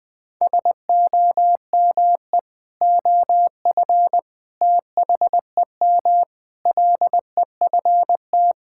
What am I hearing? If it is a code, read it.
Morse code, heard as a keyed tone, about 21 wpm, 700 Hz. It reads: SOME OF THEM LEFT
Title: Sentences with 3-Word Phrases - 20wpm